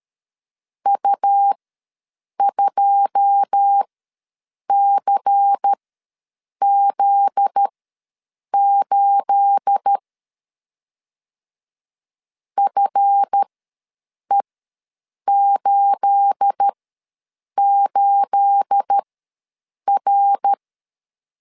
Morse code, heard as a keyed tone, about 13 wpm, spaced as for 7 wpm, 800 Hz. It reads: U2CZ8 FE88R
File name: LEZIONI / 9 28 ws 12wpm 3354